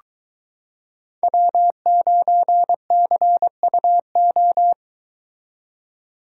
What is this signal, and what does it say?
Morse code, heard as a keyed tone, about 23 wpm, 700 Hz. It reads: W9CUO